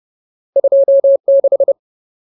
Morse code, heard as a keyed tone, about 30 wpm, 550 Hz. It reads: 26
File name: Ultimate Word List / 30WPM Ultimate